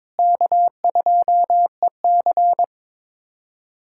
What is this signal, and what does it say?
Morse code, heard as a keyed tone, about 22 wpm, 700 Hz. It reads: K2EC
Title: USA Callsign DE Callsign - 4 Characters - 22wpm